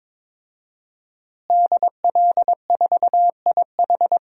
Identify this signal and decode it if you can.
Morse code, heard as a keyed tone, about 22 wpm, 700 Hz. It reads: DL4IH
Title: DXCC-100 Call Signs - 5 Characters - Repeated 3x - 22wpm